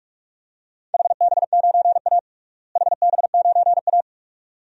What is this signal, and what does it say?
Morse code, heard as a keyed tone, about 46 wpm, 700 Hz. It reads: HB9A HB9A
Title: DXCC-100 Call Signs - 4 Characters - Repeated 2x - 45wpm